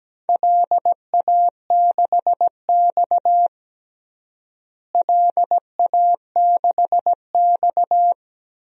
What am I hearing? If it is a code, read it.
Morse code, heard as a keyed tone, about 17 wpm, 700 Hz. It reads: LA6X LA6X
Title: DXCC-100 Call Signs - 4 Characters - Repeated 2x - 17wpm